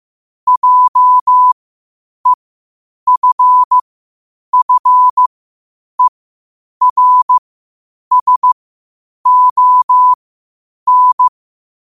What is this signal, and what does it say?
Morse code, heard as a keyed tone, about 15 wpm, spaced as for 8 wpm, 1000 Hz. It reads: JEFFERSON
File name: qotdfiles / 8wpm